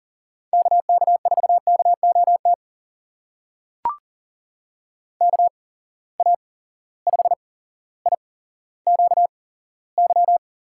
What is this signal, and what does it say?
Morse code, heard as a keyed tone, about 40 wpm, spaced as for 17 wpm, 700 Hz. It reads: KK4KOT KA5IQY
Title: USA Call Signs - 6 Characters - 8x Character Spacing - 40wpm